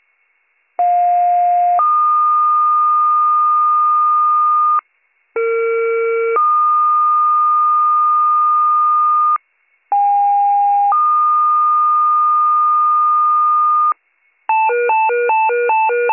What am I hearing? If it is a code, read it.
Two-tone page, heard as a one-second tone, then a three-second tone, about 710 then 1220 Hz, about 460 then 1220 Hz, about 810 then 1220 Hz. It is heard from 0.8 to 13.9 s.